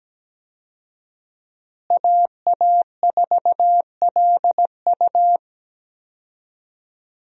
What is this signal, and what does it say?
Morse code, heard as a keyed tone, about 17 wpm, 700 Hz. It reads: AA4LU
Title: USA Call Signs - 5 Characters - Repeated 3x - 17wpm